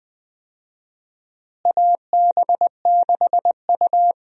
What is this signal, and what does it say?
Morse code, heard as a keyed tone, about 20 wpm, 700 Hz. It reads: AB6U
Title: USA Call Signs - 4 Characters - Repeated 3x - 20wpm